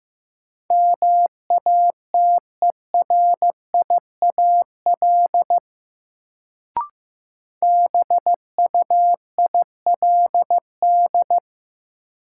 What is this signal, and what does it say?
Morse code, heard as a keyed tone, about 15 wpm, 700 Hz. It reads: MATERIAL BUILD
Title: Sentences from Top 1100 Words - New Word Drill Alt - 15wpm